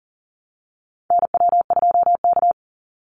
Morse code, heard as a keyed tone, about 40 wpm, 700 Hz. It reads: NW2K
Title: Call Signs Repeated 3x - 40wpm